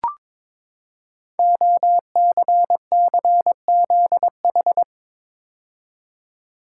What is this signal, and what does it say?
Morse code, heard as a keyed tone, about 22 wpm, 700 Hz. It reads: OCCZH
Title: Code Groups - 5 Letters - 22wpm